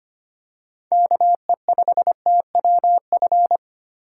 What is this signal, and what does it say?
Morse code, heard as a keyed tone, about 25 wpm, 700 Hz. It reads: KE5TWF
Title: USA Call Signs - 6 Characters - 4x Character Spacing - 25wpm